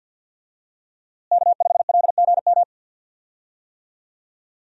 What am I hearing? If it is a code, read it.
Morse code, heard as a keyed tone, about 50 wpm, 700 Hz. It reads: K5LGM
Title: USA Call Signs - 5 Characters - Repeated 2x - 50wpm